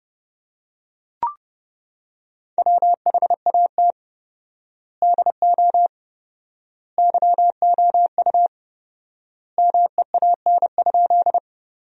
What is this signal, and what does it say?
Morse code, heard as a keyed tone, about 30 wpm, 700 Hz. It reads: WHAT DO YOU MEAN?
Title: 4x Word-Spacing - Sentences from Top 200 Words - 30wpm